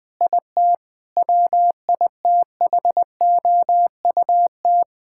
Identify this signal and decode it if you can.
Morse code, heard as a keyed tone, about 20 wpm, 700 Hz. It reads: IT WITHOUT
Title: Difficult Sentences with Hints - 20wpm